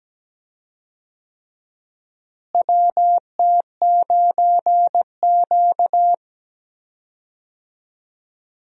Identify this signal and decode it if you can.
Morse code, heard as a keyed tone, about 17 wpm, 700 Hz. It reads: WT9Q
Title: USA Call Signs - 4 Characters - Repeated 3x - 17wpm